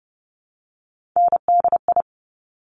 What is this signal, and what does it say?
Morse code, heard as a keyed tone, about 30 wpm, 700 Hz. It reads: NDI
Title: Top Three Letter Combinations in Words 30wpm